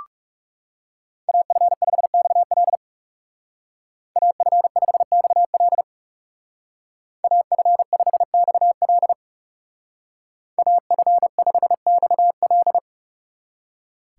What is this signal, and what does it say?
Morse code, heard as a keyed tone, about 39 wpm, 700 Hz. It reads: AF5XL AF5XL AF5XL AF5XL